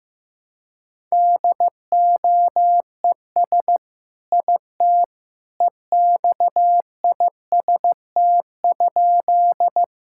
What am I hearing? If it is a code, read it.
Morse code, heard as a keyed tone, about 16 wpm, 700 Hz. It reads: DOES IT EXIST?
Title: Sentences from Top 1000 Words - Review Part 5 - 15wpm